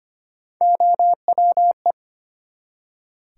Morse code, heard as a keyed tone, about 25 wpm, 700 Hz. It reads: OWE
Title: Three Letter Words - 25wpm